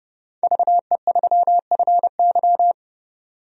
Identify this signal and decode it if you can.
Morse code, heard as a keyed tone, about 30 wpm, 700 Hz. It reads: VE3FY
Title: DXCC-100 Call Signs - 5 Characters - 30wpm